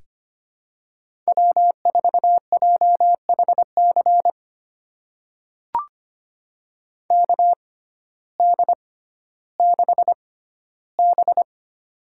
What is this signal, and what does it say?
Morse code, heard as a keyed tone, about 25 wpm, spaced as for 14 wpm, 700 Hz. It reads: W4JHC KD6B
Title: USA Call Signs - 6x Character Spacing - 25wpm